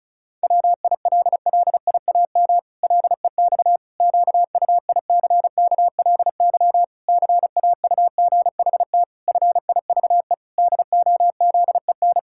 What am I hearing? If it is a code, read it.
Morse code, heard as a keyed tone, about 36 wpm, 700 Hz. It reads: WILLIAM LEX QUICKLY CAUGHT FIVE DOZEN
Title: Pangram Challenge v2 - 35wpm